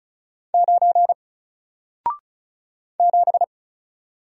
Morse code, heard as a keyed tone, about 36 wpm, 700 Hz. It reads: 9 7
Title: Single Letter-Number Alt 35wpm